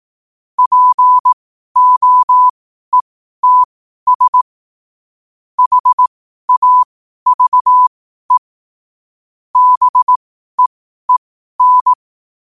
Morse code, heard as a keyed tone, about 18 wpm, spaced as for 12 wpm, 1000 Hz. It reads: POETS HAVE BEEN
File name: qotdfiles / 12wpm